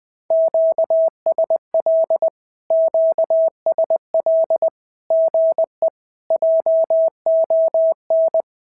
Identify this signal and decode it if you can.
Morse code, heard as a keyed tone, about 19 wpm, 650 Hz. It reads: QSL QSL GE JON
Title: SOTA QSO Practice 20wpm